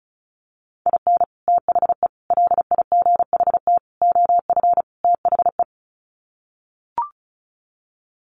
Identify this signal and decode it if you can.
Morse code, heard as a keyed tone, about 34 wpm, 700 Hz. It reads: IN THE LIGHT OF THE
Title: Sets of 5 Words Encore2 35wpm